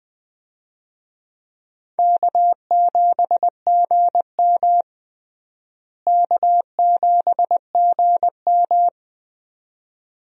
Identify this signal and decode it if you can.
Morse code, heard as a keyed tone, about 20 wpm, 700 Hz. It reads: K7GM K7GM
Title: USA Call Signs - 4 Characters - Repeated 2x - 20wpm